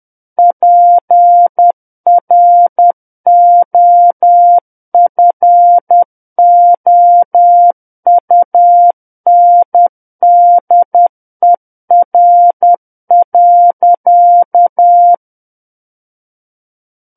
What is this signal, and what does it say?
Morse code, heard as a keyed tone, about 10 wpm, 700 Hz. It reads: PROFOUNDER.
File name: Book One / 01-Chapter 1 - 10 WPM